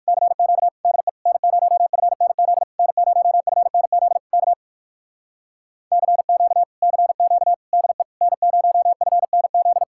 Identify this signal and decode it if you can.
Morse code, heard as a keyed tone, about 59 wpm, 700 Hz. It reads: CQ DE N0FNZ N0FNZ K CQ CQ DE N0FNZ